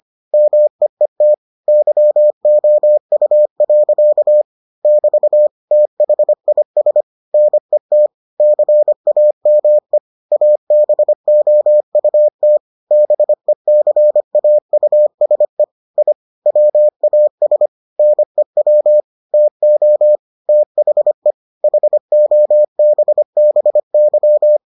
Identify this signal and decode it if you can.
Morse code, heard as a keyed tone, about 25 wpm, 600 Hz. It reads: MEET YOU. <BT> THIS NET CAME ABOUT BECAUSE I WAS NEW TO THE HOBBY